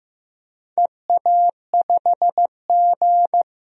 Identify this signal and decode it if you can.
Morse code, heard as a keyed tone, about 15 wpm, 700 Hz. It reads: EA5G
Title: DXCC-100 Call Signs - 4 Characters - 15wpm